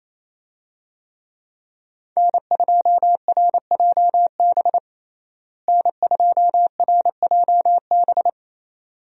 Morse code, heard as a keyed tone, about 27 wpm, 700 Hz. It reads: N2RJB N2RJB
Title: USA Call Signs - 5 Characters - Repeated 2x - 28wpm